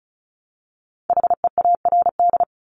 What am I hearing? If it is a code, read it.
Morse code, heard as a keyed tone, about 36 wpm, 700 Hz. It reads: HEARD
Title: Multi-letter Phonemes Repeated 3x 35wpm